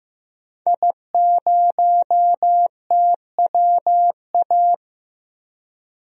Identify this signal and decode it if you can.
Morse code, heard as a keyed tone, about 15 wpm, 700 Hz. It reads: I0TWA
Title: DXCC-100 Call Signs - 5 Characters - Repeated 3x - 15wpm